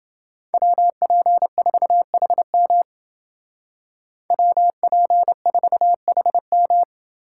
Morse code, heard as a keyed tone, about 28 wpm, 700 Hz. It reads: WP4HM WP4HM